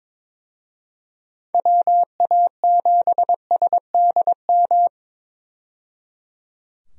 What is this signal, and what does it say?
Morse code, heard as a keyed tone, about 22 wpm, 700 Hz. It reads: WA7SDM
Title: USA Call Signs - 6 Characters - 22wpm